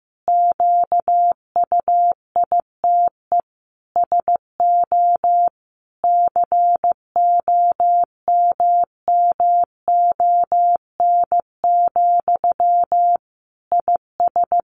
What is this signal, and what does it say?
Morse code, heard as a keyed tone, about 15 wpm, 700 Hz. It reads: QUITE SO COMMON, IS
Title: Sentences from Top 700 Words 15wpm